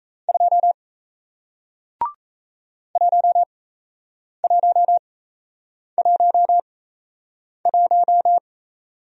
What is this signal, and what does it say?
Morse code, heard as a keyed tone, about 38 wpm, spaced as for 8 wpm, 700 Hz. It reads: 2 1111